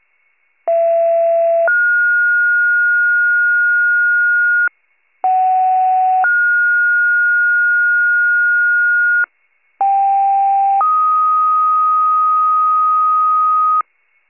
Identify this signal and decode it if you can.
Two-tone page, heard as a one-second tone, then a three-second tone, about 670 then 1430 Hz, about 750 then 1430 Hz, about 790 then 1250 Hz, with noise. It is heard from 0.7 to 13.8 s.
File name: wcalert